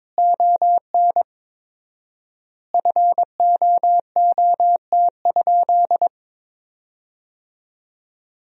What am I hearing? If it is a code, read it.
Morse code, heard as a keyed tone, about 22 wpm, 700 Hz. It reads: ON FOOT?
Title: 4x Word-Spacing - Sentences from Top 500 Words - 22wpm